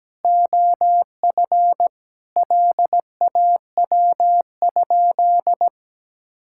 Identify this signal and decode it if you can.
Morse code, heard as a keyed tone, about 17 wpm, 700 Hz. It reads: OF LAW?
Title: Sentences from Top 1600 Words 17wpm